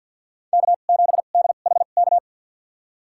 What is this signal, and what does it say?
Morse code, heard as a keyed tone, about 52 wpm, spaced as for 36 wpm, 700 Hz. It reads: K7DHK
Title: USA Call Signs - 5 Characters - 2x Character Spacing - 50wpm